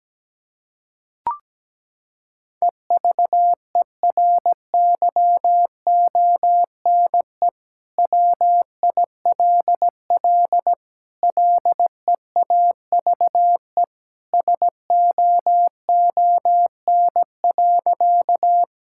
Morse code, heard as a keyed tone, about 17 wpm, 700 Hz. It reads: EVERYONE WILL LEAVE SOON.
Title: Sentences from Top 1000 Words - Review Part 5 - 17wpm